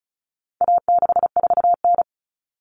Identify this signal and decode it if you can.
Morse code, heard as a keyed tone, about 34 wpm, 700 Hz. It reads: A64N
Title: Common Cut Numbers - 4-Digit 35wpm